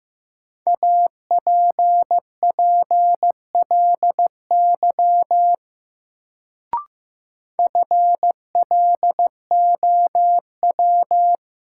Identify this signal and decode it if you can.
Morse code, heard as a keyed tone, about 15 wpm, 700 Hz. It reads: APPLY FLOW
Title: Sentences from Top 1200 Words - New Word Drill Alt - 15wpm